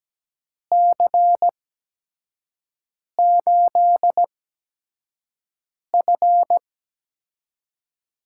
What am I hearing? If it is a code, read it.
Morse code, heard as a keyed tone, about 17 wpm, spaced as for 5 wpm, 700 Hz. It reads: C8F
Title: USA Call Signs - 6 Characters - 8x Character Spacing - 17wpm